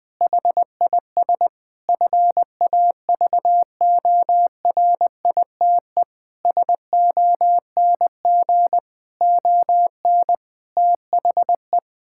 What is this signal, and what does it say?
Morse code, heard as a keyed tone, about 21 wpm, 700 Hz. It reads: HIS FAVORITE SONG ON THE
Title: Difficult Sentences with Hints - 20wpm